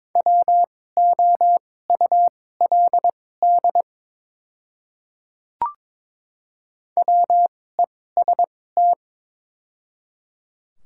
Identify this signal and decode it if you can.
Morse code, heard as a keyed tone, about 22 wpm, spaced as for 16 wpm, 700 Hz. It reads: WOULD WEST
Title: 2x Character Spacing - Top 500 Words - 22wpm